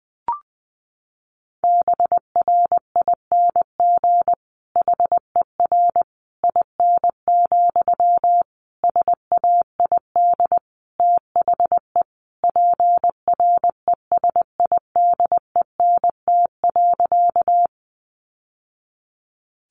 Morse code, heard as a keyed tone, about 20 wpm, 700 Hz. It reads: BRING HER IN, SAID THE PRESIDENT.
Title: Sentences from Top 1000 Words 20wpm